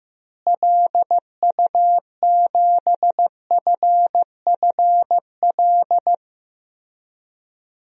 Morse code, heard as a keyed tone, about 15 wpm, 700 Hz. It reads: LU7FFL
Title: DXCC-100 Call Signs - 6 Characters - Repeated 3x - 15wpm